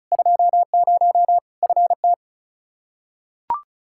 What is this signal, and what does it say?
Morse code, heard as a keyed tone, about 34 wpm, 700 Hz. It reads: 20 FT
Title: Practice QSO - Exchange 2 - Info - 35wpm